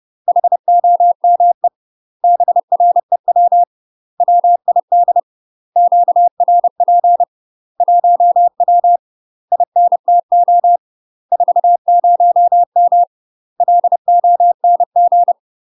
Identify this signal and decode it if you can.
Morse code, heard as a keyed tone, about 30 wpm, 700 Hz. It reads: HOME BREW WID QRP 1W INTO 40M LONG
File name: UK MASTER QSOs TRAINING / 30WPM / UK MASTER QSOs 0005